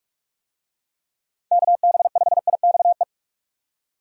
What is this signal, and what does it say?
Morse code, heard as a keyed tone, about 46 wpm, 700 Hz. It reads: KB5IXE
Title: USA Call Signs - 6 Characters - 45wpm